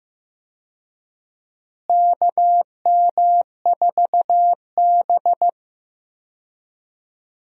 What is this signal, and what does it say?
Morse code, heard as a keyed tone, about 15 wpm, 700 Hz. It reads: KM4B
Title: USA Call Signs - 4 Characters - Repeated 3x - 15wpm